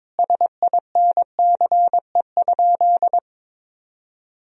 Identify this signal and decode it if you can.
Morse code, heard as a keyed tone, about 22 wpm, 700 Hz. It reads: SINCE?
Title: Sentences from Top 1000 Words - Review Part 1 - 22wpm